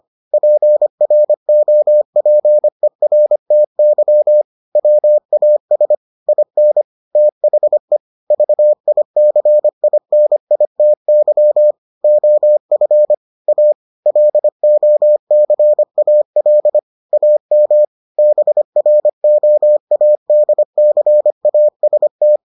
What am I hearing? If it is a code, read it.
Morse code, heard as a keyed tone, about 25 wpm, 600 Hz. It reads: PROPERTY WAS IN THE VICINITY OF A LOCAL AM BROADCAST